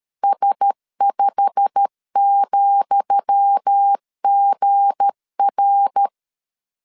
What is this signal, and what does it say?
Morse code, heard as a keyed tone, about 13 wpm, 800 Hz. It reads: S5,GR
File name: LEZIONI / 10 SIGNS / 10D COMA 12wpm 2544